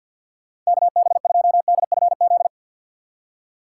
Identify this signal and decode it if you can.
Morse code, heard as a keyed tone, about 53 wpm, 700 Hz. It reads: KB2DFZ
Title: USA Call Signs - 3x Character Spacing - 50wpm